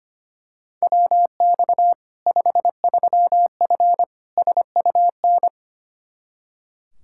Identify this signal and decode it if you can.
Morse code, heard as a keyed tone, about 25 wpm, 700 Hz. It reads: WX 53F SUN
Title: Practice QSO - Exchange 2 - Info - 25wpm